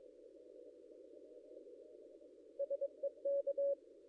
Morse code, heard as a keyed tone, about 22 wpm, 560 Hz. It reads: SEK
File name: contests / ss-cw-k5tr-op-n5rz -0800